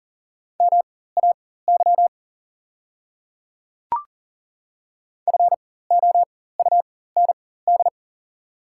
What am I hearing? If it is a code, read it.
Morse code, heard as a keyed tone, about 38 wpm, spaced as for 18 wpm, 690 Hz. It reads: MAY FOUND
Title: 4x Character Spacing - Top 300 Words - 40wpm